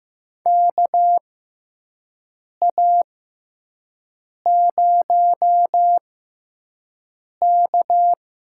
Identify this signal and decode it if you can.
Morse code, heard as a keyed tone, about 15 wpm, spaced as for 5 wpm, 700 Hz. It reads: KA0K
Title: USA Call Signs - 6 Characters - 6x Character Spacing - 15wpm